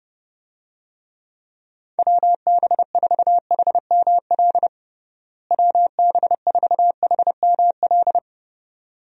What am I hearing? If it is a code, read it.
Morse code, heard as a keyed tone, about 30 wpm, 700 Hz. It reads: WB4HML WB4HML
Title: USA Call Signs - 6 Characters - Repeated 2x - 30wpm